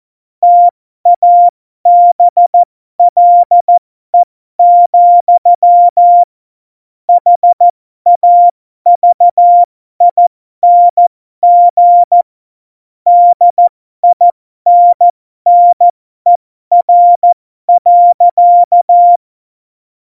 Morse code, heard as a keyed tone, about 14 wpm, spaced as for 12 wpm, 700 Hz. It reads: TABLE, HAVING DINNER.
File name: Morsekurs / Witze